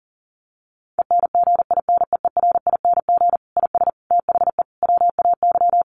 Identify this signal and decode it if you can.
Morse code, heard as a keyed tone, about 40 wpm, 700 Hz. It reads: ENGINEERING IS THE WAY
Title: Sentences from Top 1500 Words 40wpm